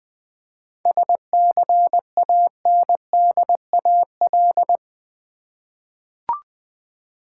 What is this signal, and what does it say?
Morse code, heard as a keyed tone, about 20 wpm, 700 Hz. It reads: SCANDAL
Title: 7 Letter Words - First 3 Letters Given - 20wpm